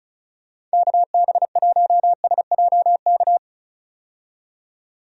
Morse code, heard as a keyed tone, about 36 wpm, 700 Hz. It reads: KB1SJK
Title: USA Call Signs - 6 Characters - 35wpm